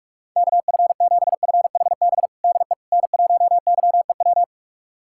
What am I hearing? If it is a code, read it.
Morse code, heard as a keyed tone, about 48 wpm, 700 Hz. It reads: KF7FHB DE N1YEW
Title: USA Callsign DE Callsign 45wpm